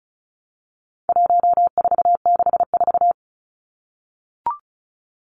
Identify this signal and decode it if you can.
Morse code, heard as a keyed tone, about 34 wpm, 700 Hz. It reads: 1464
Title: Numbers - 4-Digit 35wpm